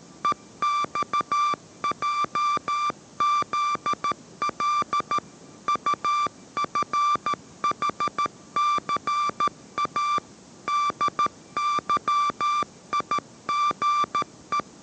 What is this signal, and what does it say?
Morse code, heard as a keyed tone, about 13 wpm, 1180 Hz. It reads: EXJZL UFHCA DYIGE